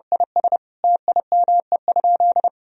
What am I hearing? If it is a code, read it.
Morse code, heard as a keyed tone, about 30 wpm, 700 Hz. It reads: IS TIME?
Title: Sentences from Top 1000 Words - Review Part 3 - 30wpm